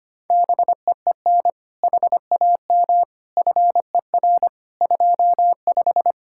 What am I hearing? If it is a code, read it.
Morse code, heard as a keyed tone, about 24 wpm, 700 Hz. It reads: BEEN HAM FER 25
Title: Speed-Racing Common QSO Phrases Encore - 25wpm